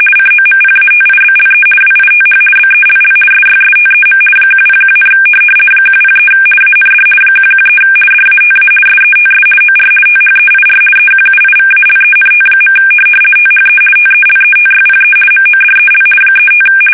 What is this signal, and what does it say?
RTTY, about 50 baud. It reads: COMPATIBLE WITH BELL SYSTEM 101, 103 AND 113 DATA SETS OR THEIR EQUIVALENT. THE TELETYPE MODEL 35
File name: fsk850-50B / teletype-corp